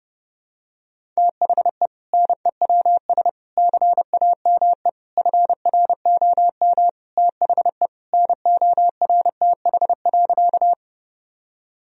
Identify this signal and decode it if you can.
Morse code, heard as a keyed tone, about 32 wpm, 700 Hz. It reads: THE NEWS CAME FROM THE NORTH.
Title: Sentences from Top 1000 Words - Review Part 1 - 30wpm